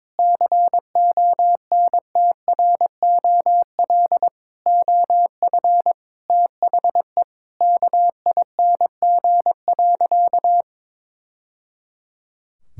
Morse code, heard as a keyed tone, about 22 wpm, 700 Hz. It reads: CONTROL OF THE KING.
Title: Sentences from Top 1000 Words - Review Part 5 - 22wpm